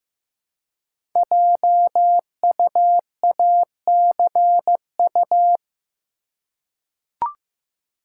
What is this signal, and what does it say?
Morse code, heard as a keyed tone, about 15 wpm, 700 Hz. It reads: JUACU
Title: Code Groups - 5 Letters - 15wpm